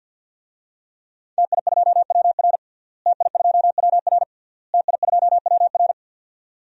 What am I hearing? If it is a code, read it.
Morse code, heard as a keyed tone, about 50 wpm, 700 Hz. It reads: TI2WR TI2WR TI2WR
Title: DXCC-100 Call Signs - 5 Characters - Repeated 3x - 50wpm